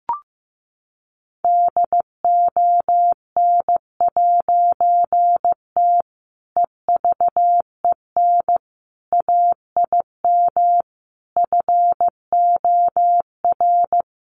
Morse code, heard as a keyed tone, about 15 wpm, 700 Hz. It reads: DON'T EVEN AIM FOR
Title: A Sound Of Thunder 15wpm